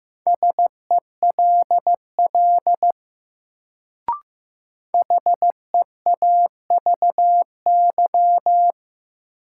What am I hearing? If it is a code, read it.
Morse code, heard as a keyed tone, about 15 wpm, 700 Hz. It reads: SELL HEAVY
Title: Sentences from Top 1100 Words - New Word Drill Alt - 15wpm